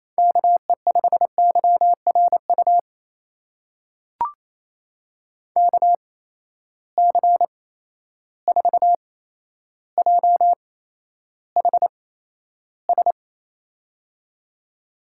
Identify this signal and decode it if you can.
Morse code, heard as a keyed tone, about 28 wpm, spaced as for 12 wpm, 700 Hz. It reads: KE5YRU KC4JHS
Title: USA Call Signs - 6 Characters - 8x Character Spacing - 28wpm